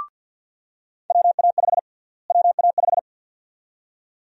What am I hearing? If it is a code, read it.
Morse code, heard as a keyed tone, about 50 wpm, 700 Hz. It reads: WA5 WA5
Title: USA Call Sign Prefixes - Repeated 2x - 50wpm